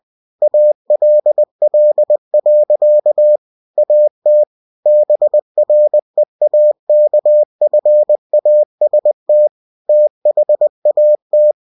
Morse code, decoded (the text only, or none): ALL. AT BREAKFAST THAT